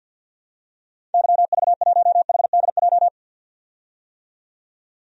Morse code, heard as a keyed tone, about 52 wpm, 700 Hz. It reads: YV1HDJ
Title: DXCC-100 Call Signs - 6 Characters - 50wpm